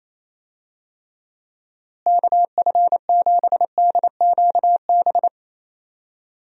Morse code, heard as a keyed tone, about 28 wpm, 700 Hz. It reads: KF7DQB
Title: USA Call Signs - 6 Characters - Repeated 2x - 28wpm